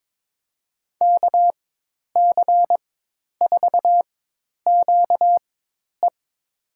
Morse code, heard as a keyed tone, about 22 wpm, spaced as for 10 wpm, 700 Hz. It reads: KC4QE
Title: USA Call Signs - 5 Characters - 4x Character Spacing - 22wpm